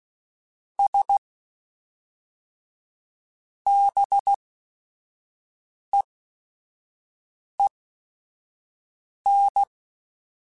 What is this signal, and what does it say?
Morse code, heard as a keyed tone, about 16 wpm, spaced as for 5 wpm, 800 Hz. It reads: S BEEN
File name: Technical Subjects / Technical Subjects-mp3s / AND Gate @5